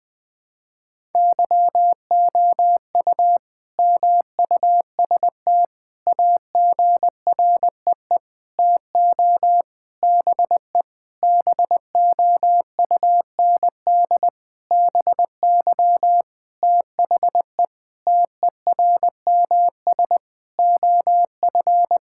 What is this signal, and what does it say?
Morse code, heard as a keyed tone, about 20 wpm, 700 Hz. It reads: YOU MUST AGREE TO BE BOUND BY THE TERMS OF
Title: Sentences with 9-Word Phrases - 20wpm